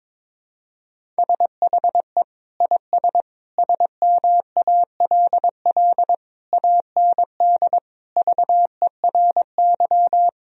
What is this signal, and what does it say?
Morse code, heard as a keyed tone, about 22 wpm, 700 Hz. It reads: SHE IS SMALL AND VERY
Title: Sentences from Top 1000 Words - Review Part 1 - 22wpm